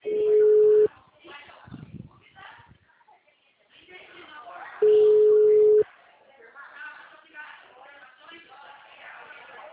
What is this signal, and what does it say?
Ringback tone (Europe, 425 Hz), with speech.